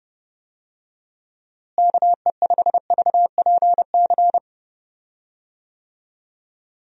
Morse code, heard as a keyed tone, about 32 wpm, 700 Hz. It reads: KE5VPC
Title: USA Call Signs - 6 Characters - Repeated 3x - 30wpm